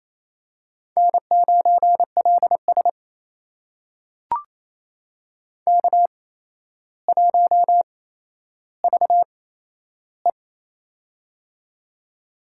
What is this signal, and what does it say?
Morse code, heard as a keyed tone, about 27 wpm, spaced as for 12 wpm, 700 Hz. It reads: N9LS K1VE
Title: USA Call Signs - 4 Characters - 8x Character Spacing - 28wpm